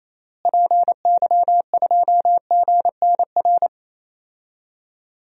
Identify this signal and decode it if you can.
Morse code, heard as a keyed tone, about 27 wpm, 700 Hz. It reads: PY2GNR
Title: DXCC-100 Call Signs - 6 Characters - Repeated 2x - 28wpm